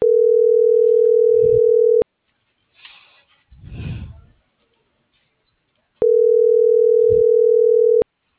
Ringback tone (North America), 440 + 480 Hz.